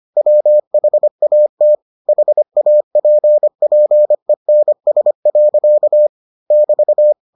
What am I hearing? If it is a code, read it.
Morse code, heard as a keyed tone, about 25 wpm, 600 Hz. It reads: WHAT HAPPENS. <BT>